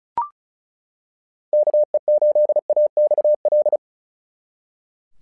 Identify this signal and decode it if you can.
Morse code, heard as a keyed tone, about 36 wpm, 600 Hz. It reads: KE8AXL
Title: SOTA QSO Practice 35wpm